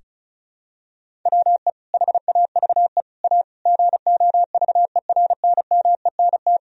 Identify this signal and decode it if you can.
Morse code, heard as a keyed tone, about 34 wpm, 700 Hz. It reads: WE HAVE A GOVERNMENT
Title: Sentences from Top 1000 Words - Review Part 1 - 35wpm